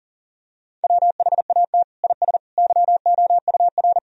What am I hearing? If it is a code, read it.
Morse code, heard as a keyed tone, about 40 wpm, 700 Hz. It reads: WHAT IS YOUR
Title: Sentences from Top 1000 Words - Review Part 4 - 40wpm